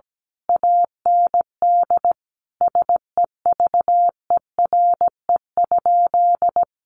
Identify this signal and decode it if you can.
Morse code, heard as a keyed tone, about 17 wpm, 700 Hz. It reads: AND SEVERE?
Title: Sentences from Top 1400 Words 17wpm